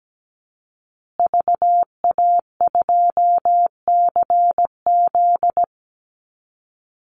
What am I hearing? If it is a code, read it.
Morse code, heard as a keyed tone, about 17 wpm, 700 Hz. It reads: VA2CZ
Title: Call Signs Repeated 3x - 17wpm